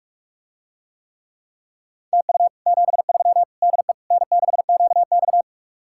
Morse code, heard as a keyed tone, about 48 wpm, 700 Hz. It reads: TU 73 DE N6QX